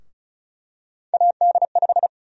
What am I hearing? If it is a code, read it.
Morse code, heard as a keyed tone, about 36 wpm, 700 Hz. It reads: AD5
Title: USA Call Sign Prefixes - Repeated 2x - 35wpm